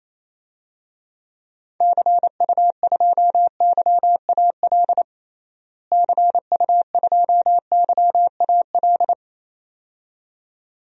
Morse code, heard as a keyed tone, about 28 wpm, 700 Hz. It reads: CU2YAL CU2YAL
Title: DXCC-100 Call Signs - 6 Characters - Repeated 2x - 28wpm